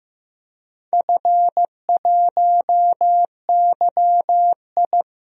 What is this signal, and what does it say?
Morse code, heard as a keyed tone, about 15 wpm, 700 Hz. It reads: F1YI
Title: DXCC-100 Call Signs - 4 Characters - Repeated 3x - 15wpm